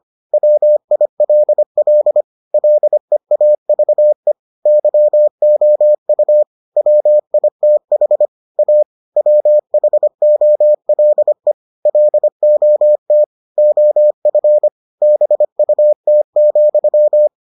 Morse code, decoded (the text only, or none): WILL LEAVE YOU WITH A WHOLE LOT OF BUT,